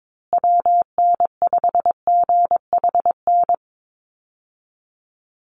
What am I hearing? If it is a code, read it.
Morse code, heard as a keyed tone, about 22 wpm, 700 Hz. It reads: WN5GHN
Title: USA Call Signs Repeated 2x 22wpm